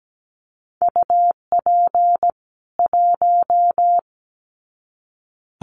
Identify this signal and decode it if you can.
Morse code, heard as a keyed tone, about 17 wpm, 700 Hz. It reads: UP 1
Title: Common QSO Phrases 17wpm